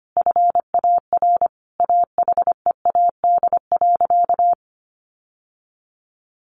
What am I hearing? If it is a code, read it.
Morse code, heard as a keyed tone, about 25 wpm, 700 Hz. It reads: FAR AHEAD.
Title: Sentences from Top 1100 Words 25wpm